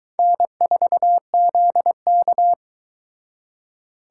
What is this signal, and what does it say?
Morse code, heard as a keyed tone, about 22 wpm, 700 Hz. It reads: N4ZK